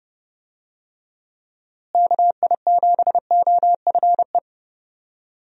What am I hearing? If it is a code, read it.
Morse code, heard as a keyed tone, about 30 wpm, 700 Hz. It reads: KI7OFE
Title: USA Call Signs - 6 Characters - Repeated 3x - 30wpm